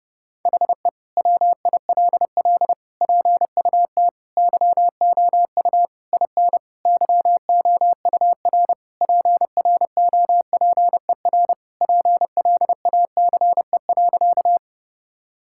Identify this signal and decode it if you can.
Morse code, heard as a keyed tone, about 32 wpm, 700 Hz. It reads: HE WILL PUT YOU IN YOUR PROPER PLACE.
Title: Sentences from Top 1000 Words - Review Part 5 - 30wpm